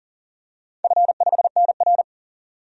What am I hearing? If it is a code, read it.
Morse code, heard as a keyed tone, about 40 wpm, 700 Hz. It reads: F5NR
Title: DXCC-100 Call Signs - 4 Characters - 40wpm